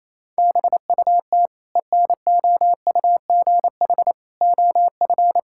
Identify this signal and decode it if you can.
Morse code, heard as a keyed tone, about 28 wpm, 700 Hz. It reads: BUT ENOUGH OF
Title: Sentences from Top 1000 Words - Review Part 6 - 28wpm